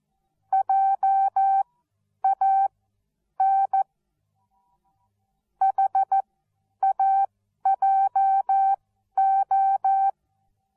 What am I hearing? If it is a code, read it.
Morse code, heard as a keyed tone, about 14 wpm, spaced as for 10 wpm, 780 Hz. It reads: JAN HAJO
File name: Onlinelehrgang / morsen / DJ4UF Morsen13